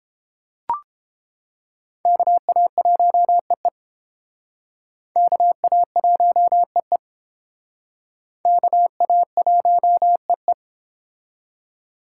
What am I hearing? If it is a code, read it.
Morse code, heard as a keyed tone, about 31 wpm, 700 Hz. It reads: KA1EE KA1EE KA1EE